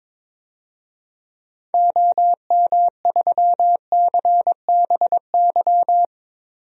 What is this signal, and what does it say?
Morse code, heard as a keyed tone, about 21 wpm, 700 Hz. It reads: OM3CBY
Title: DXCC-100 Call Signs - 6 Characters - 22wpm